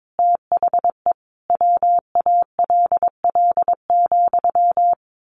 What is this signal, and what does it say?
Morse code, heard as a keyed tone, about 22 wpm, 700 Hz. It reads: THE WALL,
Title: A Sound Of Thunder 22wpm